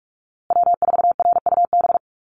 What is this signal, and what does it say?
Morse code, heard as a keyed tone, about 44 wpm, 700 Hz. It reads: W4RUB
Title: USA Call Signs Repeated 2x 45wpm